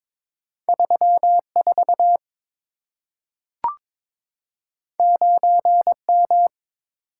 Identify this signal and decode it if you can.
Morse code, heard as a keyed tone, about 23 wpm, 700 Hz. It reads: 34 9M
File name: practice / 22wpm / Two Letter-Number Combinations - 22wpm